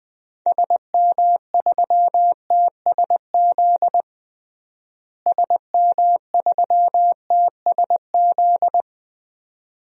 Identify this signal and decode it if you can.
Morse code, heard as a keyed tone, about 20 wpm, 690 Hz. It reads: SM3TSZ SM3TSZ
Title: DXCC-100 Call Signs - 6 Characters - Repeated 2x - 20wpm